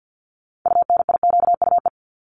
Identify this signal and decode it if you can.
Morse code, heard as a keyed tone, about 48 wpm, 700 Hz. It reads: UNIQUE
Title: Top 751-1000 Words 50wpm